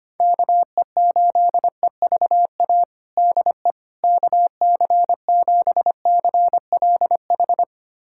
Morse code, heard as a keyed tone, about 25 wpm, 700 Hz. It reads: KE8EVA DE KC7CLH
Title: USA Callsign DE Callsign 25wpm